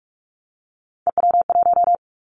Morse code, heard as a keyed tone, about 46 wpm, 690 Hz. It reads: EW1
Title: Callsign Prefixes for Top 50 DXCC - 45wpm